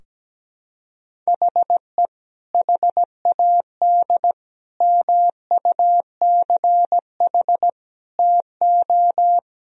Morse code, heard as a keyed tone, about 17 wpm, 700 Hz. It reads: HE HAD MUCH TO
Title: Sentences from Top 1000 Words - Review Part 4 - 17wpm